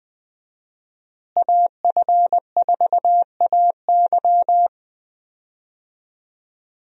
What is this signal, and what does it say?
Morse code, heard as a keyed tone, about 20 wpm, 700 Hz. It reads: AF4AY
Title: USA Call Signs - 5 Characters - Repeated 3x - 20wpm